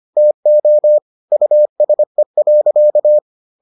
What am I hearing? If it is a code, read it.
Morse code, heard as a keyed tone, about 25 wpm, 600 Hz. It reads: TO USE.